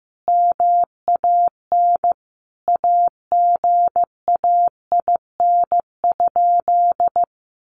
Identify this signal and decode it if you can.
Morse code, heard as a keyed tone, about 15 wpm, 700 Hz. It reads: MAN AGAIN?
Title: Sentences from Top 1400 Words 15wpm